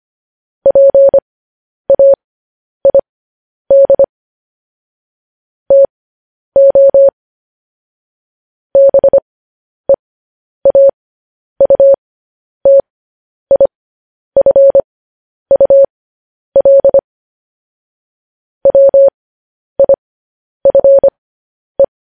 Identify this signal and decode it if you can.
Morse code, heard as a keyed tone, about 25 wpm, spaced as for 10 wpm, 550 Hz. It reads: PAID TO BEAUTIFUL WIFE
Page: News Headlines in Morse Code at 10 WPM